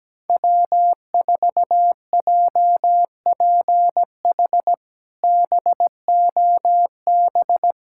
Morse code, heard as a keyed tone, about 17 wpm, 700 Hz. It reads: W4JPH BOB
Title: K1USN SST QSO Practice - 17wpm